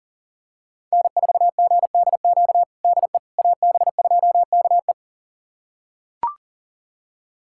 Morse code, heard as a keyed tone, about 40 wpm, 700 Hz. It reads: N4GDQ DE AB2KE
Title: USA Callsign DE Callsign - 5 Characters - 40wpm